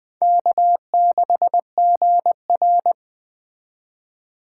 Morse code, heard as a keyed tone, about 20 wpm, 700 Hz. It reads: K6GR